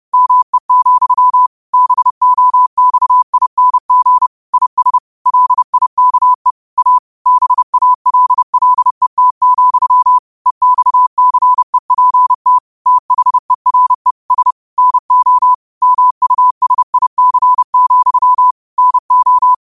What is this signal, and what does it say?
Morse code, heard as a keyed tone, about 30 wpm, 1000 Hz. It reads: ME, BOXING IS LIKE A BALLET, EXCEPT THERE S NO MUSIC, NO